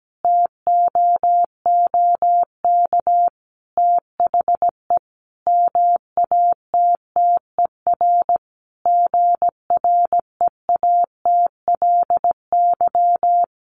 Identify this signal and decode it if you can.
Morse code, heard as a keyed tone, about 17 wpm, 700 Hz. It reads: TOOK THE MATTER GREATLY
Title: Sentences from Top 1400 Words 17wpm